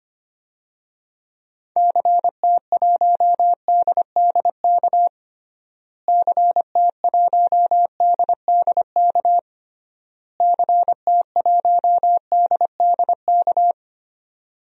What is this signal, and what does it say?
Morse code, heard as a keyed tone, about 25 wpm, 700 Hz. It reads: CT1DDK CT1DDK CT1DDK
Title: DXCC-100 Call Signs - 6 Characters - Repeated 3x - 25wpm